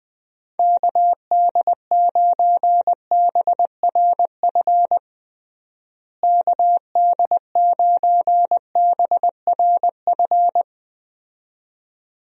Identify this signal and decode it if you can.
Morse code, heard as a keyed tone, about 20 wpm, 700 Hz. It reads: KD9BRF KD9BRF
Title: USA Call Signs - 6 Characters - Repeated 2x - 20wpm